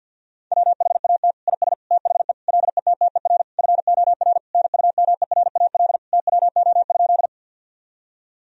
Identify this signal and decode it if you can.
Morse code, heard as a keyed tone, about 50 wpm, 700 Hz. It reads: WHAT IS THE LETTER FOR NUMERAL TWO?
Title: Sentences from Top 1000 Words - Review Part 2 - 50wpm